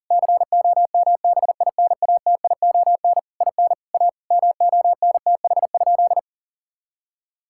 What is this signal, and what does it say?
Morse code, heard as a keyed tone, about 40 wpm, 700 Hz. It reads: COMBINATION IN A MONTH?
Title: Sentences from Top 1000 Words - Review Part 6 - 40wpm